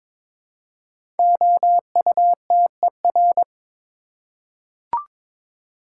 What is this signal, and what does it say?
Morse code, heard as a keyed tone, about 23 wpm, 700 Hz. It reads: OUTER
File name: 5 Letter Words - First 3 Letters Given - 22wpm